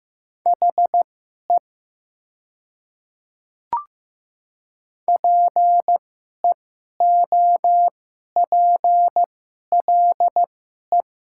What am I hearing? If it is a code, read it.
Morse code, heard as a keyed tone, about 15 wpm, spaced as for 11 wpm, 700 Hz. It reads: HE PEOPLE
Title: 2x Character Spacing - Top 100 Words - 15wpm